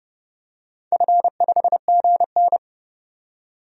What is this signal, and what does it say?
Morse code, heard as a keyed tone, about 30 wpm, 700 Hz. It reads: F5GN
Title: DXCC-100 Call Signs - 4 Characters - 30wpm